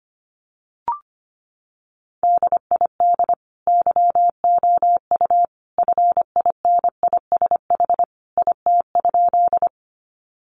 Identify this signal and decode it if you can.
Morse code, heard as a keyed tone, about 25 wpm, 700 Hz. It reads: DID YOU FINISH IT?
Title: Sentences from Top 1200 Words 25wpm